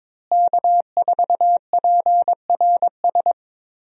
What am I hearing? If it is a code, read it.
Morse code, heard as a keyed tone, about 22 wpm, 700 Hz. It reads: K4PRS
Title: USA Call Signs - 5 Characters - Repeated 3x - 22wpm